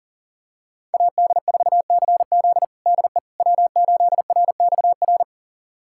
Morse code, heard as a keyed tone, about 40 wpm, 700 Hz. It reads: AD4CZ DE W8RXR
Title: USA Callsign DE Callsign - 5 Characters - 40wpm